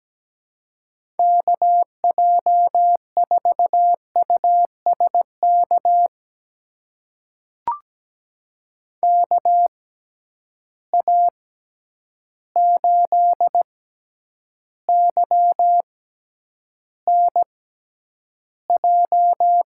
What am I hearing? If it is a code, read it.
Morse code, heard as a keyed tone, about 17 wpm, spaced as for 9 wpm, 700 Hz. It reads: KJ4USK KA8YNJ
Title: USA Call Signs - 6 Characters - 6x Character Spacing - 17wpm